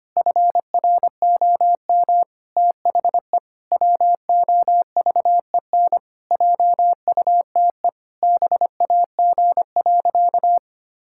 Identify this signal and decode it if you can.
Morse code, heard as a keyed tone, about 25 wpm, 700 Hz. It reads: FROM THE WOVEN JUTE BAG.
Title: Pangram Challenge v2 - 25wpm